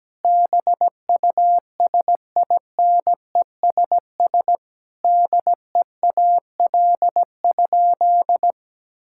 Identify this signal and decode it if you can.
Morse code, heard as a keyed tone, about 17 wpm, 700 Hz. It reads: BUSINESS DEAL?
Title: Sentences from Top 1000 Words - Review Part 6 - 17wpm